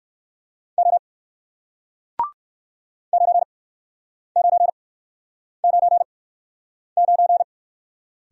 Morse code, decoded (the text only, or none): O 9999